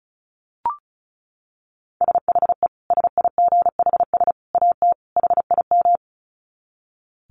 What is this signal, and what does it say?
Morse code, heard as a keyed tone, about 34 wpm, 700 Hz. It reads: SHE SIGHS AT HIM
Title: Dit Challenge 35wpm